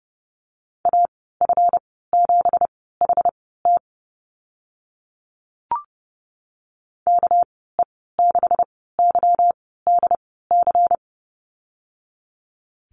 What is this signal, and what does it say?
Morse code, heard as a keyed tone, about 29 wpm, spaced as for 17 wpm, 700 Hz. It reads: AF7HT KE6YDC
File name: Lesson 37g - Letter x - Call Signs 30wpm